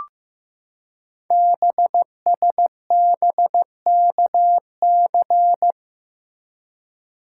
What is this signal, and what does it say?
Morse code, heard as a keyed tone, about 15 wpm, 700 Hz. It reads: BSBKC
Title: Code Groups - 5 Letters - 15wpm